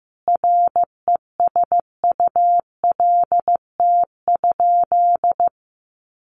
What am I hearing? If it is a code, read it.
Morse code, heard as a keyed tone, about 15 wpm, 700 Hz. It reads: RESULT?
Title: Sentences from Top 1500 Words 15wpm